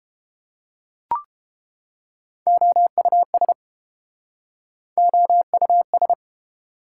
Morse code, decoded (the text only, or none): OUS OUS